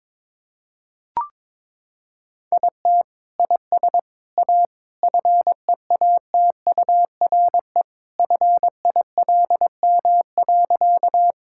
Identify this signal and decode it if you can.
Morse code, heard as a keyed tone, about 21 wpm, 700 Hz. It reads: IT IS A FEATURE FILM.
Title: Sentences from Top 1000 Words - Review Part 5 - 22wpm